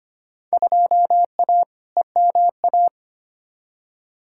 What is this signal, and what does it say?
Morse code, heard as a keyed tone, about 25 wpm, 700 Hz. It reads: 2A EMA
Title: Field Day Exchanges Encore - 25wpm